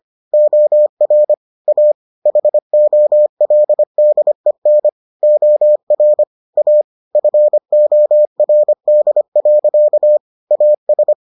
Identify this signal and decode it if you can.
Morse code, heard as a keyed tone, about 25 wpm, 600 Hz. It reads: OR A HOLDEN OR A FORD. AS